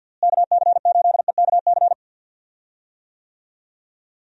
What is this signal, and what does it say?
Morse code, heard as a keyed tone, about 45 wpm, 700 Hz. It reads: KC8EKC
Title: USA Call Signs - 6 Characters - Repeated 2x - 50wpm